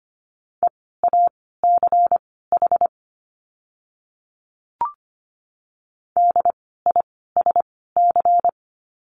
Morse code, heard as a keyed tone, about 25 wpm, spaced as for 16 wpm, 700 Hz. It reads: EACH DISC
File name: course / 25:10wpm / 094 Lesson 16f - Letter c - Words 25wpm